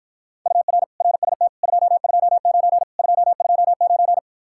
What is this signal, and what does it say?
Morse code, heard as a keyed tone, about 53 wpm, 700 Hz. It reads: UR RST 229 229